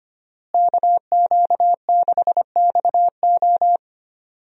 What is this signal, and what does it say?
Morse code, heard as a keyed tone, about 25 wpm, 700 Hz. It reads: KQ6XO
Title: USA Call Signs - 5 Characters - Repeated 2x - 25wpm